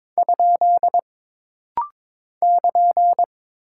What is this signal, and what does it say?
Morse code, heard as a keyed tone, about 22 wpm, 700 Hz. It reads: ? <KN>
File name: practice / 22wpm / Single Punctuation-Prosign - Rapid-Fire - 22wpm